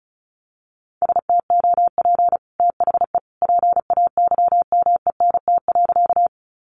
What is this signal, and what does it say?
Morse code, heard as a keyed tone, about 34 wpm, 700 Hz. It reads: STOP THE PAYMENT.
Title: Sentences from Top 1200 Words 35wpm